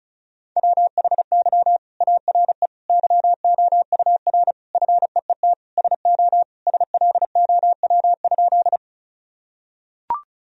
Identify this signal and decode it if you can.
Morse code, heard as a keyed tone, about 36 wpm, 700 Hz. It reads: WHY ARE YOUR FEET SO SLOW?
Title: Sentences from Top 1000 Words - Review Part 2 - 35wpm